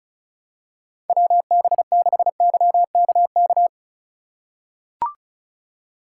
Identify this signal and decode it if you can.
Morse code, heard as a keyed tone, about 34 wpm, 700 Hz. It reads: WB6YKK
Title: USA Call Signs - 6 Characters - 4x Character Spacing - 35wpm